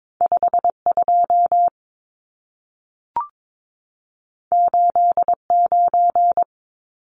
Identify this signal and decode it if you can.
Morse code, heard as a keyed tone, about 22 wpm, 700 Hz. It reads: 52 89
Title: Numbers - 2-Digit 22wpm